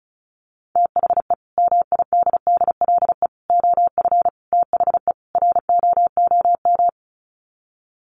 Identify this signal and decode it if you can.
Morse code, heard as a keyed tone, about 34 wpm, 700 Hz. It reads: THE MIDDLE OF THE ROOM
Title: Sets of 5 Words Encore2 35wpm